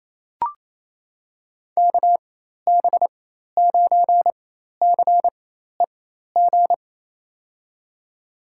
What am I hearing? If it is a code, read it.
Morse code, heard as a keyed tone, about 28 wpm, spaced as for 13 wpm, 700 Hz. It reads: KB9CEG
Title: USA Call Signs - 6 Characters - 4x Character Spacing - 28wpm